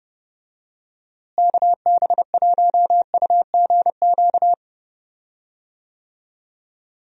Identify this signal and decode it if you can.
Morse code, heard as a keyed tone, about 30 wpm, 700 Hz. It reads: KB1UGQ
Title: USA Call Signs - 6 Characters - Repeated 3x - 30wpm